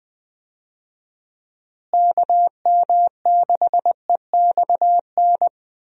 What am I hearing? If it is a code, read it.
Morse code, heard as a keyed tone, about 20 wpm, 700 Hz. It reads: KM6EXN
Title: USA Call Signs - 6 Characters - Repeated 2x - 20wpm